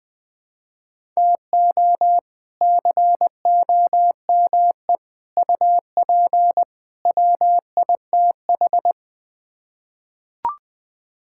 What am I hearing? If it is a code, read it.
Morse code, heard as a keyed tone, about 20 wpm, 700 Hz. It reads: TO COME UP WITH
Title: Build A Phrase - 4-Word Phrases - 20wpm